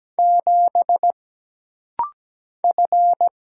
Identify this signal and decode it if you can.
Morse code, heard as a keyed tone, about 17 wpm, 700 Hz. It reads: 7 F
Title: Single Letter-Number - Rapid-Fire 17wpm